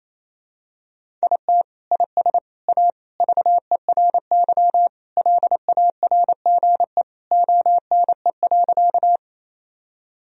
Morse code, heard as a keyed tone, about 28 wpm, 700 Hz. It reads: IT IS A VERY LARGE ONE.
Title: Sentences from Top 1000 Words - Review Part 1 - 28wpm